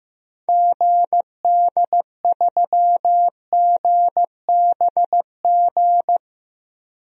Morse code, heard as a keyed tone, about 15 wpm, 700 Hz. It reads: GD3GBG
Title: DXCC-100 Call Signs - 6 Characters - Repeated 3x - 15wpm